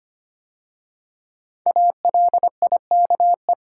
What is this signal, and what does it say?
Morse code, heard as a keyed tone, about 25 wpm, 700 Hz. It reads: ALIKE
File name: 5 Letter Words - First 3 Letters Given - 25wpm